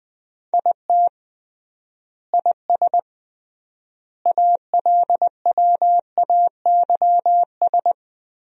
Morse code, heard as a keyed tone, about 20 wpm, 700 Hz. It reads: IT IS ALWAYS
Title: 3x Word-Spacing - Sentences from Top 300 Words - 20wpm